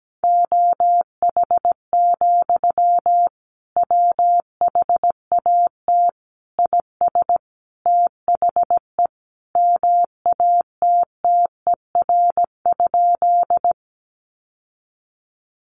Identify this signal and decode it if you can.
Morse code, heard as a keyed tone, about 17 wpm, 700 Hz. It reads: OH, WHAT IS THE MATTER?
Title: Sentences from Top 700 Words 17wpm